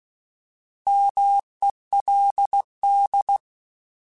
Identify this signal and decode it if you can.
Morse code, heard as a keyed tone, about 16 wpm, 800 Hz. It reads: MELD
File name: Alphabetical / Alphabetical mp3s / Ms @15